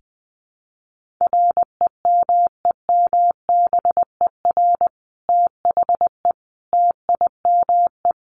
Morse code, heard as a keyed tone, about 19 wpm, 700 Hz. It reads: REMEMBER THE TIME
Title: Sentences from Top 1400 Words 20wpm